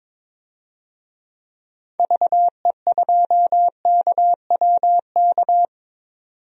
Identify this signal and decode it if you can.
Morse code, heard as a keyed tone, about 22 wpm, 700 Hz. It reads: VE2KWK
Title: DXCC-100 Call Signs - 6 Characters - 22wpm